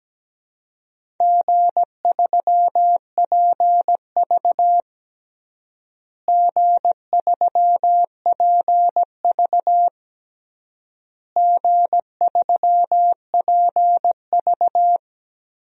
Morse code, heard as a keyed tone, about 17 wpm, 700 Hz. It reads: G3PV G3PV G3PV
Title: DXCC-100 Call Signs - 4 Characters - Repeated 3x - 17wpm